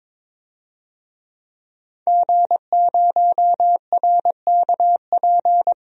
Morse code, heard as a keyed tone, about 22 wpm, 700 Hz. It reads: G0RKP
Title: DXCC-100 Call Signs - 5 Characters - 22wpm